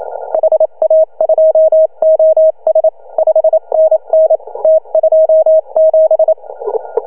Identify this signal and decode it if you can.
Morse code, heard as a keyed tone, about 26 wpm, 620 Hz. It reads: HA2OS 5NN T27
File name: CW / HA2OS RHOB 2017 CW